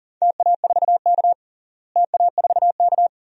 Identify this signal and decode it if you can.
Morse code, heard as a keyed tone, about 40 wpm, 700 Hz. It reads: TA4K TA4K
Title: DXCC-100 Call Signs - 4 Characters - Repeated 2x - 40wpm